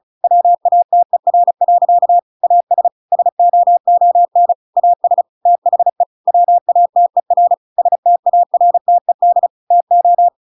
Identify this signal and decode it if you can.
Morse code, heard as a keyed tone, about 35 wpm, 700 Hz. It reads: WATER. AS SOON AS THE WATER STARTED TO